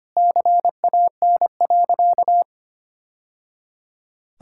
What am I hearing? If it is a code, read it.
Morse code, heard as a keyed tone, about 25 wpm, 700 Hz. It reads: CAN.
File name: practice / 25wpm / Sentences with 3-Word Phrases - 25wpm